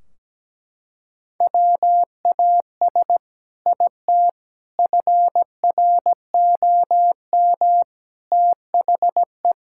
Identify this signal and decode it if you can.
Morse code, heard as a keyed tone, about 17 wpm, 700 Hz. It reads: WAS IT FROM THE
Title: Sentences from Top 1600 Words 17wpm